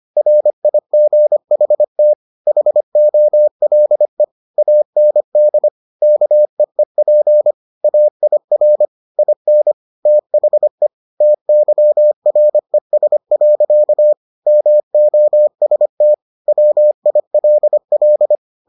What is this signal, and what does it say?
Morse code, heard as a keyed tone, about 25 wpm, 600 Hz. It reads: RIGHT HOLE AND KEEP AIR IN THE TYRES. MOST WILL